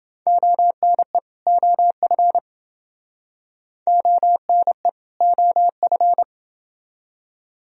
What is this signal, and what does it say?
Morse code, heard as a keyed tone, about 29 wpm, 700 Hz. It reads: ONE OF ONE OF